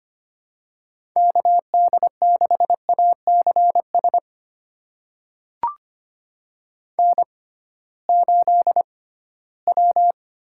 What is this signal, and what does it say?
Morse code, heard as a keyed tone, about 25 wpm, spaced as for 16 wpm, 700 Hz. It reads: KD6ACS N8W
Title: USA Call Signs - 6x Character Spacing - 25wpm